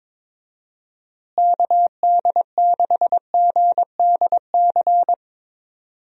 Morse code, heard as a keyed tone, about 21 wpm, 700 Hz. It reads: KD6GDC
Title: USA Call Signs - 6 Characters - Repeated 2x - 22wpm